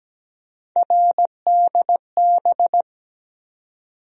Morse code, heard as a keyed tone, about 17 wpm, 700 Hz. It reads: RDB
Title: USA Call Sign Postfixes - 3 Letters 17wpm